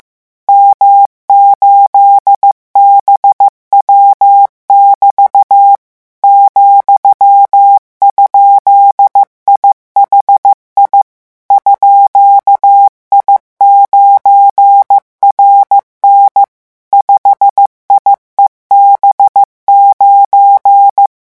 Morse code, heard as a keyed tone, about 15 wpm, 800 Hz. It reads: M8BW<BT> ,?IHI _I9RN 5IEB9
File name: morse mp3 / revision17-vitesse 14 mots